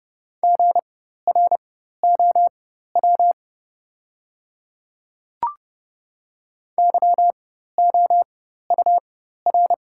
Morse code, heard as a keyed tone, about 30 wpm, spaced as for 14 wpm, 700 Hz. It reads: GROW YOUR
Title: 4x Character Spacing - Top 500 Words - 30wpm